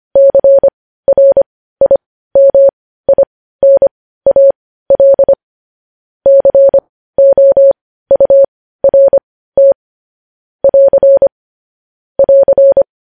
Morse code, heard as a keyed tone, about 25 wpm, spaced as for 15 wpm, 550 Hz. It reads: CRIMINAL COURT <AR> <AR>
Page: news headlines in morse code at 15 wpm